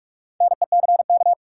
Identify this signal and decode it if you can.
Morse code, heard as a keyed tone, about 46 wpm, 700 Hz. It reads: NECK
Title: Four Letter Words - 45wpm